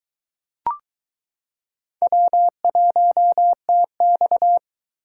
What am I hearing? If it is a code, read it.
Morse code, heard as a keyed tone, about 22 wpm, 700 Hz. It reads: W1TX